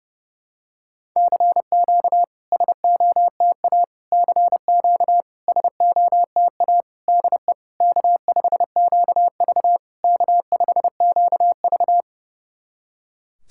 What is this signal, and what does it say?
Morse code, heard as a keyed tone, about 30 wpm, 700 Hz. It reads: CQ SOTA CQ SOTA DE K5QV K5QV
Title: SOTA QSO Practice 30wpm